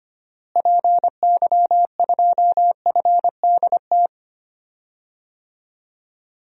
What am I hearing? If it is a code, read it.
Morse code, heard as a keyed tone, about 25 wpm, 700 Hz. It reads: PY2FDT
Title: DXCC-100 Call Signs - 6 Characters - Repeated 2x - 25wpm